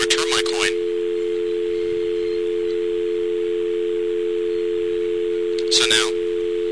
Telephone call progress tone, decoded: dial tone